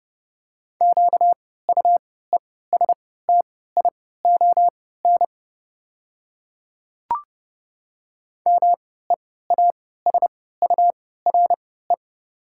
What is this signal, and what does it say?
Morse code, heard as a keyed tone, about 30 wpm, spaced as for 17 wpm, 690 Hz. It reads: QUESTION MEASURE
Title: 3x Character Spacing - Top 500 Words - 30wpm